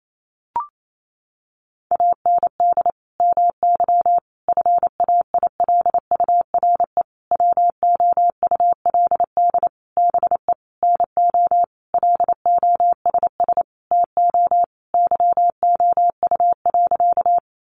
Morse code, heard as a keyed tone, about 28 wpm, 700 Hz. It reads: AND MY FAILURE WOULD BE NO LOSS TO YOU.
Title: Sentences from Top 1400 Words 28wpm